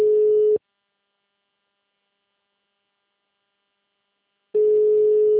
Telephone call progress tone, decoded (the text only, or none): ringback tone